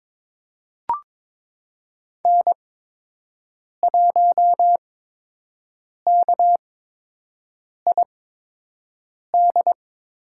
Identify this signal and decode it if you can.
Morse code, heard as a keyed tone, about 22 wpm, spaced as for 6 wpm, 700 Hz. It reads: N1KID
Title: USA Call Signs - 8x Character Spacing - 22wpm